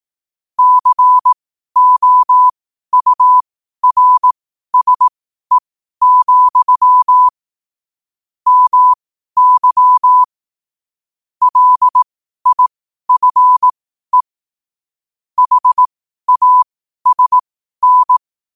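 Morse code, heard as a keyed tone, about 18 wpm, spaced as for 12 wpm, 1000 Hz. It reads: COURSE, MY LIFE HASN
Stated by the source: 12 wpm Morse.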